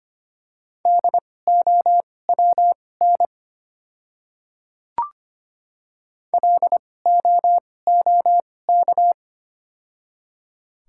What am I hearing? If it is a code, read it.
Morse code, heard as a keyed tone, about 25 wpm, spaced as for 18 wpm, 690 Hz. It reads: DOWN LOOK
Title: 2x Character Spacing - Top 100 Words - 25wpm